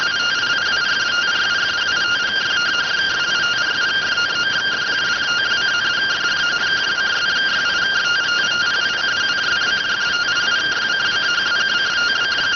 Two-tone teleprinter signal, about 170 Hz shift.